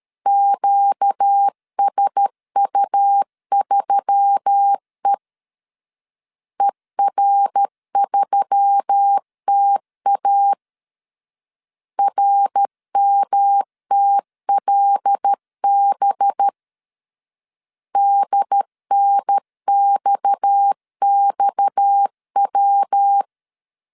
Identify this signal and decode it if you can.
Morse code, heard as a keyed tone, about 13 wpm, 800 Hz. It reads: QSU3E ER3TA RMTLB DNXXW
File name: LEZIONI / 6 PKX0 / 6 PKX 12wpm 2024